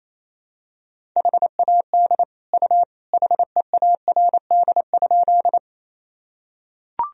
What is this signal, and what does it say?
Morse code, heard as a keyed tone, about 28 wpm, 700 Hz. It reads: HAD U HEARD?
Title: Sentences from Top 400 Words and Abbreviations 28wpm